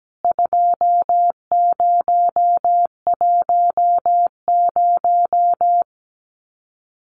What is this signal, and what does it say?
Morse code, heard as a keyed tone, about 17 wpm, 700 Hz. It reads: 2010
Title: CWT Exchange 17wpm